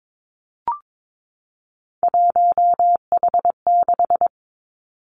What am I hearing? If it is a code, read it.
Morse code, heard as a keyed tone, about 22 wpm, 700 Hz. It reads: 1H6
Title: Three Challenging Letter-Number Combinations 22wpm